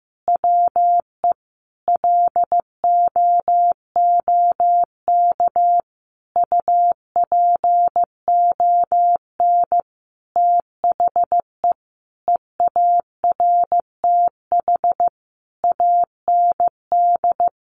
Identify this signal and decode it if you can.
Morse code, heard as a keyed tone, about 15 wpm, 700 Hz. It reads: WE LOOK UPON THE EARTH AND
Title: Sentences from Top 1200 Words 15wpm